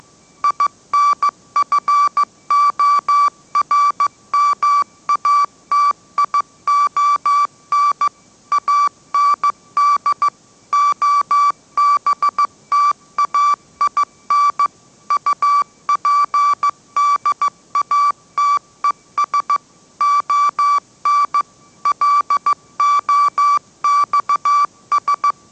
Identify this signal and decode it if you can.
Morse code, heard as a keyed tone, about 15 wpm, 1160 Hz. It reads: INFORMATION AND OBTAIN UPDATES ON LOXS
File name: mrx / mrx066 16wpm